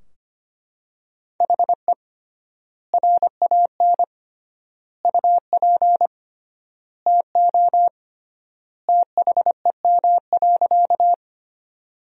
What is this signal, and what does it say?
Morse code, heard as a keyed tone, about 25 wpm, 700 Hz. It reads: HE RAN UP TO THEM.
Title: 3x Word-Spacing - Sentences from Top 500 Words - 25wpm